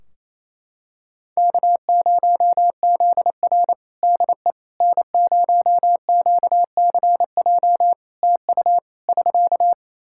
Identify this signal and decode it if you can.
Morse code, heard as a keyed tone, about 28 wpm, 700 Hz. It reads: K0ZR DE N0QCJ TU <SK>
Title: Practice QSO - Exchange 4 - Ending - 28wpm